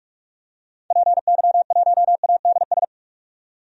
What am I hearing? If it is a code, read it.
Morse code, heard as a keyed tone, about 41 wpm, 700 Hz. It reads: PY1ADS
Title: DXCC-100 Call Signs - 6 Characters - 45wpm